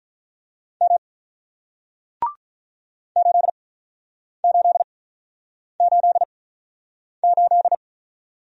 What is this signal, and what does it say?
Morse code, heard as a keyed tone, about 47 wpm, spaced as for 9 wpm, 700 Hz. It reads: M 8888